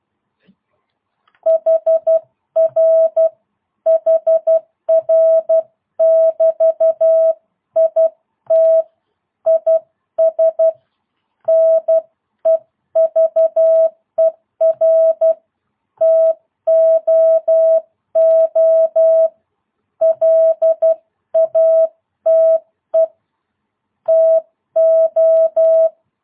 Morse code, heard as a keyed tone, about 12 wpm, 650 Hz. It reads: HR HR <BT> IT IS NEVER TOO LATE TO
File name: jh8cbh / 2022cw